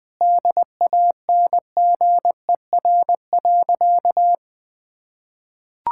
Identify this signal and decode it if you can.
Morse code, heard as a keyed tone, about 20 wpm, 700 Hz. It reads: DANGER.
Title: Sentences with 2-Word Phrases - Spoken Beginning - 20wpm